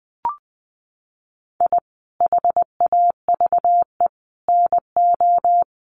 Morse code, heard as a keyed tone, about 20 wpm, 700 Hz. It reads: I HAVE NO
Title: Sentences from Top 400 Words 20wpm